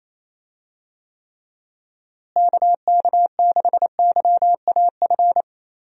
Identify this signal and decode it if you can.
Morse code, heard as a keyed tone, about 28 wpm, 700 Hz. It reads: KK6YAF
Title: USA Call Signs - 6 Characters - Repeated 3x - 28wpm